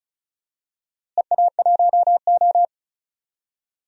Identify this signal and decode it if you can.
Morse code, heard as a keyed tone, about 36 wpm, 700 Hz. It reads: EA1O
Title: DXCC-100 Call Signs - 4 Characters - Repeated 2x - 35wpm